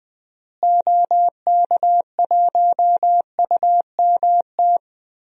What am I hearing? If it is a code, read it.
Morse code, heard as a keyed tone, about 20 wpm, 700 Hz. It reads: OK1UMT
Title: DXCC-100 Call Signs - 6 Characters - Repeated 3x - 20wpm